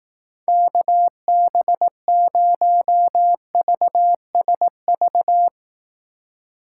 Morse code, heard as a keyed tone, about 18 wpm, 700 Hz. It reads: KB0VSV
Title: Speed Racing - USA Call Signs 15wpm